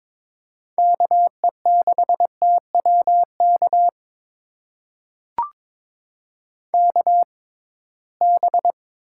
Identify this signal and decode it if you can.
Morse code, heard as a keyed tone, about 22 wpm, spaced as for 17 wpm, 700 Hz. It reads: KE6TWK KB
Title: USA Call Signs - 6 Characters - 6x Character Spacing - 22wpm